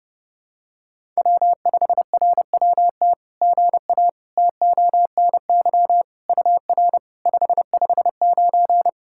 Morse code, heard as a keyed tone, about 30 wpm, 700 Hz. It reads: W5RWT GA TONY UR 559
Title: SOTA QSO Practice 30wpm